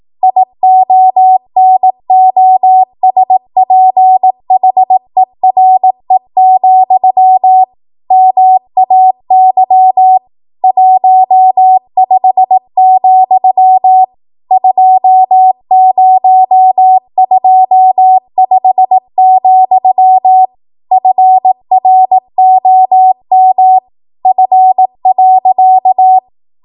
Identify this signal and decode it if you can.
Morse code, heard as a keyed tone, about 18 wpm, 750 Hz. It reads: IONOSPHERE, MAY 15, 2025, FROM F.
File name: Morse / 18WPM